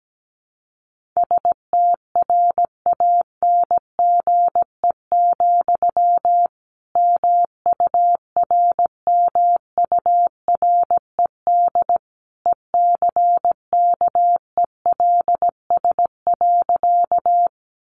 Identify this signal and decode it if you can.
Morse code, heard as a keyed tone, about 17 wpm, 700 Hz. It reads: STRANGE, MURMURED ECKELS.
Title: A Sound Of Thunder 17wpm